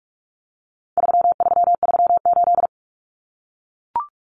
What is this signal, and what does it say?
Morse code, heard as a keyed tone, about 44 wpm, 700 Hz. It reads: 3338
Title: Common Cut Numbers - 4-Digit 45wpm